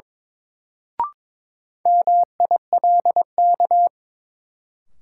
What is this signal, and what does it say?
Morse code, heard as a keyed tone, about 21 wpm, 700 Hz. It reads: MILK
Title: Sentences from Top 1500 Words - New Word Drill - 22wpm